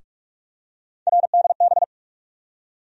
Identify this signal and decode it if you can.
Morse code, heard as a keyed tone, about 46 wpm, 700 Hz. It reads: RDB
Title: USA Call Sign Postfixes - 3 Letters 45wpm